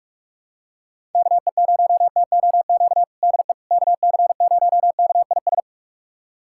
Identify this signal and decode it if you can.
Morse code, heard as a keyed tone, about 46 wpm, 700 Hz. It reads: KE0TOQ DE KC0KIS
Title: USA Callsign DE Callsign - 6 Characters - 45wpm